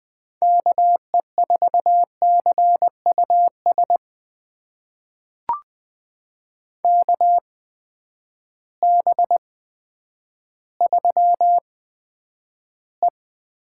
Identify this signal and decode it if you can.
Morse code, heard as a keyed tone, about 20 wpm, spaced as for 10 wpm, 700 Hz. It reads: KE4CUS KB3E
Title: USA Call Signs - 6 Characters - 8x Character Spacing - 20wpm